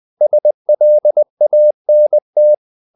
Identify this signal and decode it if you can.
Morse code, heard as a keyed tone, about 20 wpm, 600 Hz. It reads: SLANT
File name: the old man and the sea part 2 20wpm